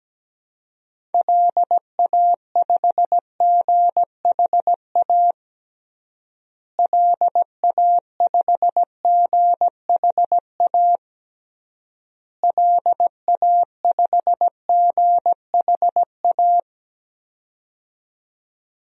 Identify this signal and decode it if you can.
Morse code, heard as a keyed tone, about 17 wpm, 700 Hz. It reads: LA5GHA LA5GHA LA5GHA
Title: DXCC-100 Call Signs - 6 Characters - Repeated 3x - 17wpm